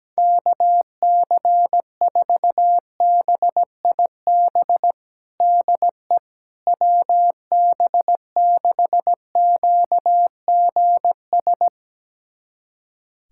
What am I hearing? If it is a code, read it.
Morse code, heard as a keyed tone, about 17 wpm, 700 Hz. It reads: KC4BIB DE WB6QGS
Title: USA Callsign DE Callsign - 6 Characters - 17wpm